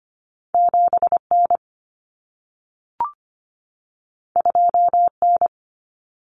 Morse code, heard as a keyed tone, about 25 wpm, 700 Hz. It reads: 7N 2N
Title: Common Cut Numbers - 2-Digit 25wpm